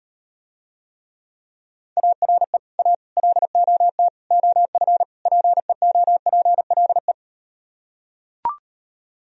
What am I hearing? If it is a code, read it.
Morse code, heard as a keyed tone, about 38 wpm, 700 Hz. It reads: ARE A LOT OF PEOPLE